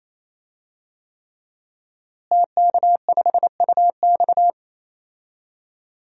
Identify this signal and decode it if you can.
Morse code, heard as a keyed tone, about 28 wpm, 700 Hz. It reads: TK5UX
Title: DXCC-100 Call Signs - 5 Characters - Repeated 3x - 28wpm